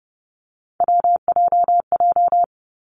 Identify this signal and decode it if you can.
Morse code, heard as a keyed tone, about 30 wpm, 700 Hz. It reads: WJJ
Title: Three Challenging Letter-Number Combinations 30wpm